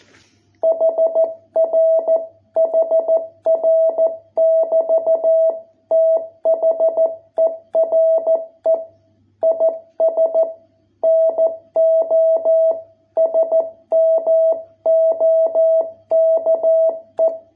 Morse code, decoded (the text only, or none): HR HR <BT> THERE IS NO SMOKE